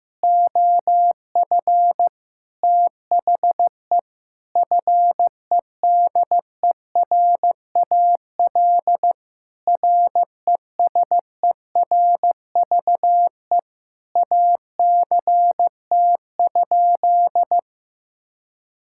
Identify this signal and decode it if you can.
Morse code, heard as a keyed tone, about 15 wpm, 700 Hz. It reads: OF THE FEDERAL RESERVE ACT?
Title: Sentences from Top 1000 Words - Review Part 6 - 15wpm